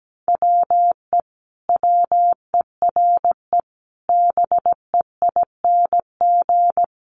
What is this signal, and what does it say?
Morse code, heard as a keyed tone, about 17 wpm, 700 Hz. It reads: WE WERE BEING
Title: Sentences from Top 1100 Words 17wpm